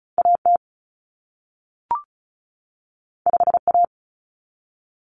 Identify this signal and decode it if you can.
Morse code, heard as a keyed tone, about 36 wpm, 700 Hz. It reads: AT 5A
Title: Common Cut Numbers - 2-Digit 35wpm